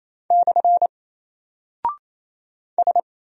Morse code, heard as a keyed tone, about 27 wpm, 700 Hz. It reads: / S
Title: Single Character (Letter:Number:Punctuation:Prosign) Alt 28wpm